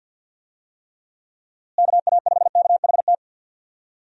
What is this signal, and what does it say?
Morse code, heard as a keyed tone, about 52 wpm, 700 Hz. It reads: KA5KHT
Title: USA Call Signs - 6 Characters - Repeated 3x - 50wpm